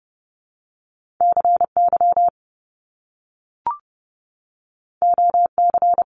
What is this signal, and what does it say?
Morse code, heard as a keyed tone, about 30 wpm, 700 Hz. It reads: CY OC
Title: Top Two Letter Combinations in Words 30wpm